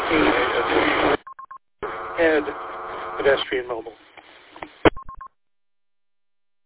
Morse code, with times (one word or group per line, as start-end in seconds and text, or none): none